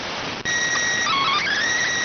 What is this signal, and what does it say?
SSTV (slow-scan television), PD 290.